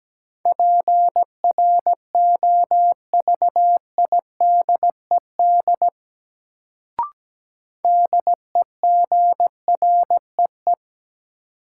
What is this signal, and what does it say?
Morse code, heard as a keyed tone, about 17 wpm, 700 Hz. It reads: PROVIDED DEGREE
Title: Sentences from Top 1500 Words - New Word Drill - 17wpm